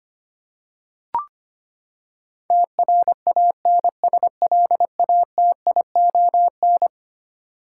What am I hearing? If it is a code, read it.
Morse code, heard as a keyed tone, about 25 wpm, 700 Hz. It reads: TRANSLATION
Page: morse-code-ninja-practice-25wpm